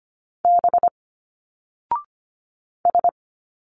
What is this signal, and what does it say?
Morse code, heard as a keyed tone, about 24 wpm, 700 Hz. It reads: B S
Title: Single Letters 25wpm